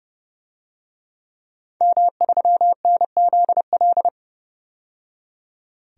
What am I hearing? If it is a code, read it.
Morse code, heard as a keyed tone, about 30 wpm, 700 Hz. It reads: M3NZL
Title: DXCC-100 Call Signs - 5 Characters - 30wpm